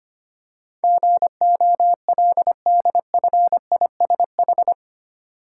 Morse code, heard as a keyed tone, about 25 wpm, 700 Hz. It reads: GOLDFISH
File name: practice / 25wpm / Easy Words with Hints - 25wpm